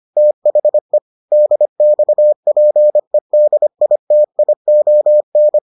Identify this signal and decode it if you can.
Morse code, heard as a keyed tone, about 25 wpm, 600 Hz. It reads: THE DXPEDITION